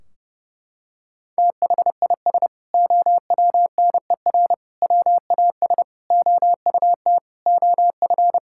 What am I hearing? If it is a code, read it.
Morse code, heard as a keyed tone, about 30 wpm, 700 Hz. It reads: THIS OWNER WAS OUT OF
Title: Sentences from Top 1600 Words 30wpm